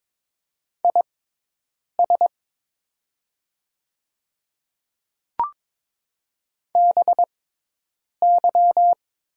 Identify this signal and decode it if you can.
Morse code, heard as a keyed tone, about 23 wpm, spaced as for 8 wpm, 700 Hz. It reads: IS BY
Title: 6x Character Spacing - Top 100 Words - 22wpm